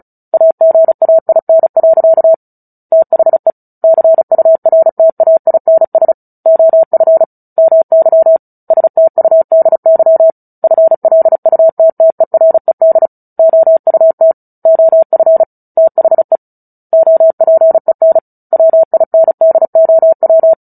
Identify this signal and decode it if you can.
Morse code, heard as a keyed tone, about 34 wpm, 650 Hz. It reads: AGAIN. THE CURTAINS OF MY STUDY FLUTTERED OUT OF THE OPEN WINDOW
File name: Book Two / 09-Chapter 9 - 35 WPM